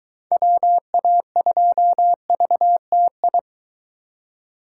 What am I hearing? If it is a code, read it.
Morse code, heard as a keyed tone, about 23 wpm, 700 Hz. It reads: WA2VTI